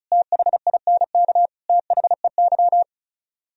Morse code, heard as a keyed tone, about 42 wpm, 690 Hz. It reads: THINK THEY
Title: Speed-Racing - Sets of 3 Words 35wpm